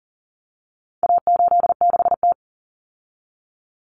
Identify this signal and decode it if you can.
Morse code, heard as a keyed tone, about 40 wpm, 700 Hz. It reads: A86T
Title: Common Cut Numbers - 4-Digit 40wpm